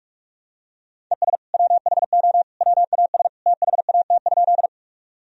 Morse code, heard as a keyed tone, about 46 wpm, 700 Hz. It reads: ES WHO WAS THAT?
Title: Sentences from Top 100 Words and Abbreviations 45wpm